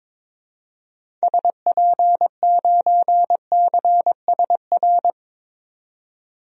Morse code, heard as a keyed tone, about 22 wpm, 700 Hz. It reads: SP9CSR
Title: DXCC-100 Call Signs - 6 Characters - 22wpm